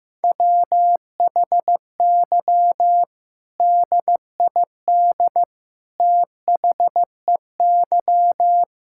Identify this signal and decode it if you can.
Morse code, heard as a keyed tone, about 16 wpm, 700 Hz. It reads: WHY DID THEY
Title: Sentences from Top 1600 Words 15wpm